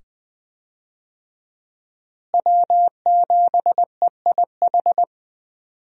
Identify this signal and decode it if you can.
Morse code, heard as a keyed tone, about 20 wpm, 700 Hz. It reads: W7EIH
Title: USA Call Signs - 5 Characters - Repeated 2x - 20wpm